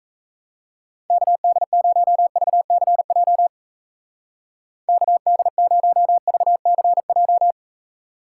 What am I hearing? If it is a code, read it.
Morse code, heard as a keyed tone, about 35 wpm, 700 Hz. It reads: KD0VCJ KD0VCJ
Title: Speed Racing - USA 2x3 Call Signs 28wpm